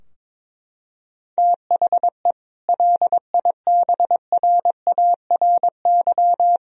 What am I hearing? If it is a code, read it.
Morse code, heard as a keyed tone, about 22 wpm, 700 Hz. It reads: THE LIBRARY
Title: Sentences from Top 1000 Words - Review Part 4 - 22wpm